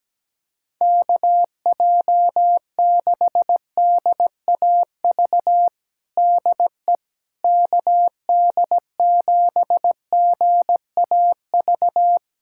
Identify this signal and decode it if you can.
Morse code, heard as a keyed tone, about 17 wpm, 700 Hz. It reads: KJ6DAV DE KD7GAV
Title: USA Callsign DE Callsign - 6 Characters - 17wpm